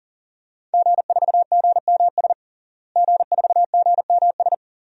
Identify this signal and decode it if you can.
Morse code, heard as a keyed tone, about 40 wpm, 700 Hz. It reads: G4GMS G4GMS
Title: DXCC-100 Call Signs - 5 Characters - Repeated 3x - 40wpm